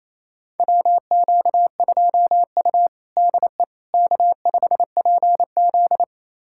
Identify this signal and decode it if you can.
Morse code, heard as a keyed tone, about 28 wpm, 700 Hz. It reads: WQ2U DE K5PZ
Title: USA Callsign DE Callsign - 4 Characters - 28wpm